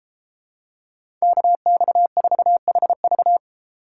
Morse code, heard as a keyed tone, about 32 wpm, 700 Hz. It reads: KX4HV